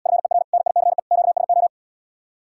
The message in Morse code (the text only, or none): FER NICE QSO